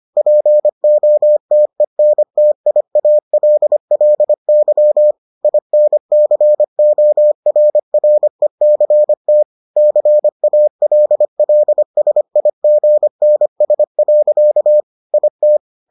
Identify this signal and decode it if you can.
Morse code, heard as a keyed tone, about 25 wpm, 600 Hz. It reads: POTENTIALLY INCORRECT CALLSIGNS. IT